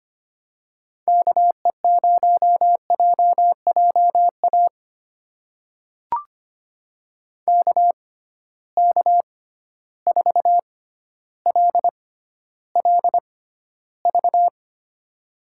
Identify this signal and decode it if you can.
Morse code, heard as a keyed tone, about 25 wpm, spaced as for 13 wpm, 700 Hz. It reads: KE0JJA KK4LLV
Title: USA Call Signs - 6 Characters - 6x Character Spacing - 25wpm